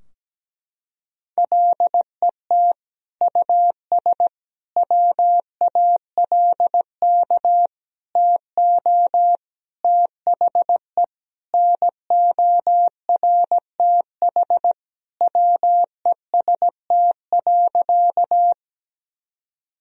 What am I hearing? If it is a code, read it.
Morse code, heard as a keyed tone, about 17 wpm, 700 Hz. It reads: LET US WALK TO THE NORTH WEST.
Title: Sentences from Top 1000 Words - Review Part 2 - 17wpm